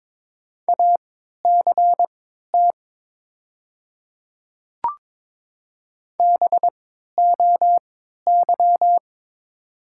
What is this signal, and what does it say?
Morse code, heard as a keyed tone, about 22 wpm, spaced as for 12 wpm, 700 Hz. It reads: ACT BOY
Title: 3x Character Spacing - Top 200 Words - 22wpm